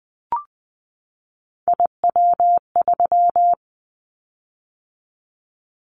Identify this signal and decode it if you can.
Morse code, heard as a keyed tone, about 20 wpm, 700 Hz. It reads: IW3
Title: Callsign Prefixes for Top 50 DXCC - 20wpm